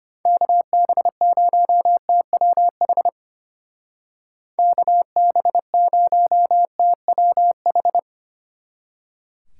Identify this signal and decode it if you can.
Morse code, heard as a keyed tone, about 33 wpm, 700 Hz. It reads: KB0TWH KB0TWH